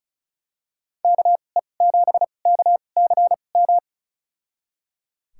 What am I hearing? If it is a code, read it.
Morse code, heard as a keyed tone, about 34 wpm, spaced as for 25 wpm, 700 Hz. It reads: KE7KCM
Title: USA Call Signs - 6 Characters - 2x Character Spacing - 35wpm